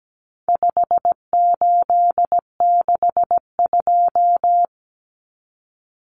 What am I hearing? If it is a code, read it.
Morse code, heard as a keyed tone, about 17 wpm, 700 Hz. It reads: 5862
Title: Common Cut Numbers - 4-Digit 17wpm